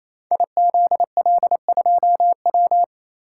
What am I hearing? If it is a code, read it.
Morse code, heard as a keyed tone, about 28 wpm, 690 Hz. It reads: IZL2W
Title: Code Groups - 5 Letter:Numbers - 28wpm